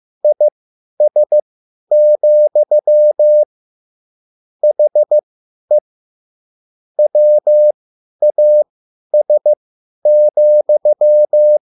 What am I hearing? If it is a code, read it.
Morse code, heard as a keyed tone, about 15 wpm, spaced as for 10 wpm, 600 Hz. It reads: IS, HE WAS,
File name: QSL / CW lesson